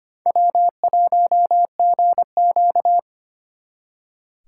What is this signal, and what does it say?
Morse code, heard as a keyed tone, about 31 wpm, 700 Hz. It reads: W1GQ